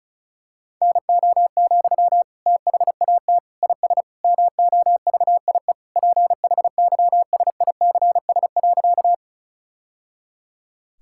Morse code, heard as a keyed tone, about 34 wpm, 700 Hz. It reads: NO, THAT IS MOVIE PHYSICS.
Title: Sentences from Top 1000 Words - Review Part 5 - 35wpm